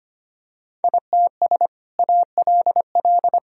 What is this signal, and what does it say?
Morse code, heard as a keyed tone, about 25 wpm, 700 Hz. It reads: ITS ALL
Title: Sentences from Top 1000 Words - Review Part 4 - 25wpm